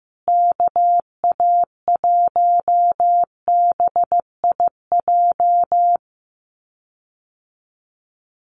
Morse code, heard as a keyed tone, about 15 wpm, 700 Hz. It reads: KA1BIJ
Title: USA Call Signs Repeated 2x 15wpm